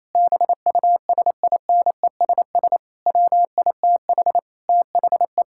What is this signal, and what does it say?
Morse code, heard as a keyed tone, about 28 wpm, 700 Hz. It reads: BUSINESS WITH THE
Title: Sentences from Top 1000 Words - Review Part 6 - 28wpm